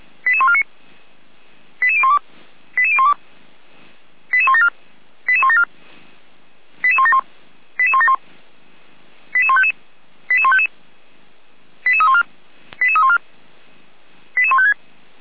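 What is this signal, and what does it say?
Five-tone selective call (ZVEI), heard as quick runs of five tones, tones of about 70 ms. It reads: 88149 88012 88012 88174 88174 88171 88171 88155 88155 88314 88314 88156
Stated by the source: Five-tone ZVEI selective call received on an SDR.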